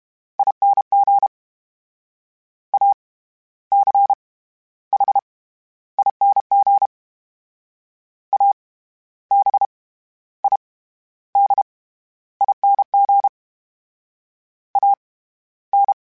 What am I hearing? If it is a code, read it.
Morse code, heard as a keyed tone, about 32 wpm, 800 Hz. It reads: ING A C H ING A B I D ING A N
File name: Special INGs / Special INGs-mp3s / SlowToFaster ing @30